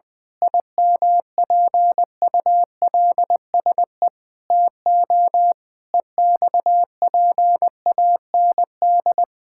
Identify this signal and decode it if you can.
Morse code, heard as a keyed tone, about 21 wpm, 700 Hz. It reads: IMPULSE TO EXPAND
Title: Pangram Challenge v2 - 20wpm